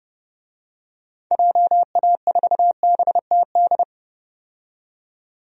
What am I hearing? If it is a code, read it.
Morse code, heard as a keyed tone, about 32 wpm, 700 Hz. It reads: JA4BTD
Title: DXCC-100 Call Signs - 6 Characters - 30wpm